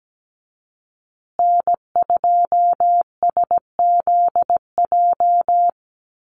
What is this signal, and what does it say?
Morse code, heard as a keyed tone, about 17 wpm, 700 Hz. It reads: N2SZJ
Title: USA Call Signs Repeated 2x 17wpm